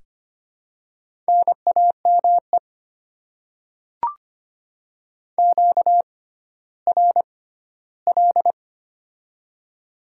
Morse code, heard as a keyed tone, about 25 wpm, spaced as for 14 wpm, 700 Hz. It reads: NAME QRL
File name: Top 20 QSO Elements - 6x Character Spacing - 25wpm